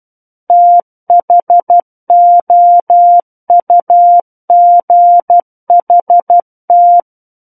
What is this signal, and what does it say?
Morse code, heard as a keyed tone, about 12 wpm, 700 Hz. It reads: THOUGHT
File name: Book One / 03-Chapter 3 - 12 WPM